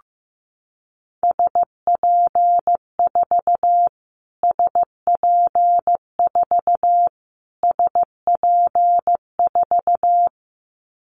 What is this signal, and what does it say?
Morse code, heard as a keyed tone, about 15 wpm, 700 Hz. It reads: SP4 SP4 SP4
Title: Callsign Prefixes for Top 25 DXCC - Repeated 3x - 15wpm